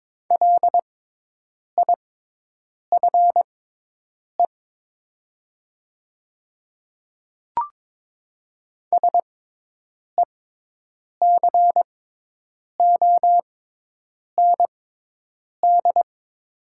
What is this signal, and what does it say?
Morse code, heard as a keyed tone, about 22 wpm, spaced as for 8 wpm, 700 Hz. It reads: LIFE SECOND
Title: 6x Character Spacing - Top 300 Words - 22wpm